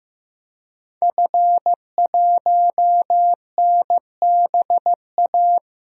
Morse code, heard as a keyed tone, about 15 wpm, 700 Hz. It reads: F1NBA
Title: DXCC-100 Call Signs - 5 Characters - Repeated 2x - 15wpm